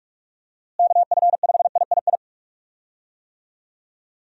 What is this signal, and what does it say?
Morse code, heard as a keyed tone, about 44 wpm, 700 Hz. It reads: KF5III
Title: USA Call Signs - 6 Characters - Repeated 3x - 45wpm